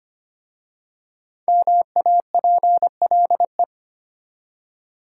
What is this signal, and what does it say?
Morse code, heard as a keyed tone, about 25 wpm, 700 Hz. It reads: MAPLE
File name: practice / 25wpm / Easy Words with Hints - 25wpm